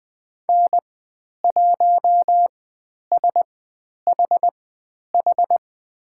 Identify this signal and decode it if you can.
Morse code, heard as a keyed tone, about 20 wpm, spaced as for 10 wpm, 700 Hz. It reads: N1SHH
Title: 102 Lesson 17g - Letter u - Call Signs 20wpm